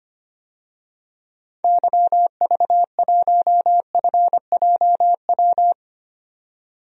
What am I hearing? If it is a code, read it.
Morse code, heard as a keyed tone, about 25 wpm, 700 Hz. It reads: YV1FJW
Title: DXCC-100 Call Signs - 6 Characters - 25wpm